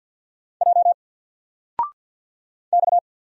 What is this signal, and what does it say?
Morse code, heard as a keyed tone, about 48 wpm, 700 Hz. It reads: J X
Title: Single Letter-Number 50wpm